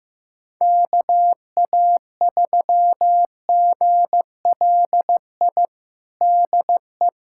Morse code, heard as a keyed tone, about 15 wpm, 700 Hz. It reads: KA3GLI DE
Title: USA Callsign DE Callsign - 6 Characters - 15wpm